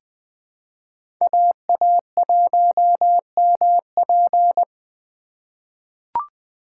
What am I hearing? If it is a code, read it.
Morse code, heard as a keyed tone, about 20 wpm, 700 Hz. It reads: AA1MP
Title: USA Call Signs - 5 Characters - 4x Character Spacing - 20wpm